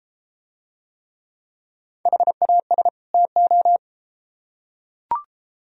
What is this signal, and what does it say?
Morse code, heard as a keyed tone, about 33 wpm, 700 Hz. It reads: HAS TO